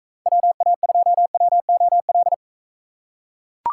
Morse code, heard as a keyed tone, about 40 wpm, 700 Hz. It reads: WA2WOL